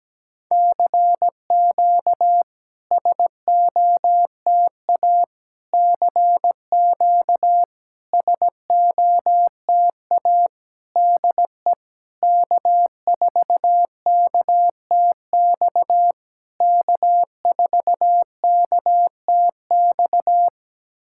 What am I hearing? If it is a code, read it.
Morse code, heard as a keyed tone, about 17 wpm, 690 Hz. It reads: CQ SOTA CQ SOTA DE K4KTX K4KTX
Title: SOTA QSO Practice 17wpm